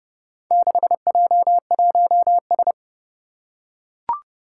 Morse code, heard as a keyed tone, about 30 wpm, 700 Hz. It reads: 6J1S
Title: Four Challenging Letter-Number Combinations 30wpm